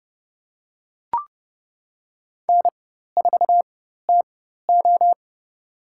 Morse code, heard as a keyed tone, about 31 wpm, spaced as for 14 wpm, 700 Hz. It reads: N4TO
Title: USA Call Signs - 4 Characters - 4x Character Spacing - 30wpm